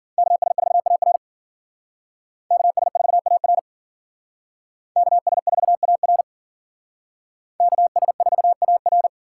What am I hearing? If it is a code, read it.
Morse code, heard as a keyed tone, about 50 wpm, 700 Hz. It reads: KS4AR KS4AR KS4AR KS4AR
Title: Speed Racing - USA 2x2 Call Signs 40wpm